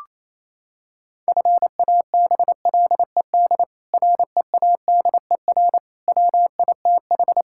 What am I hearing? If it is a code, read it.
Morse code, heard as a keyed tone, about 28 wpm, 700 Hz. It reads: FABLED READER WITH
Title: Pangram Challenge v2 - 28wpm